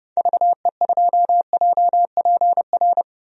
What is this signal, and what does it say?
Morse code, heard as a keyed tone, about 30 wpm, 700 Hz. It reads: VE2JPR
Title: DXCC-100 Call Signs - 6 Characters - 30wpm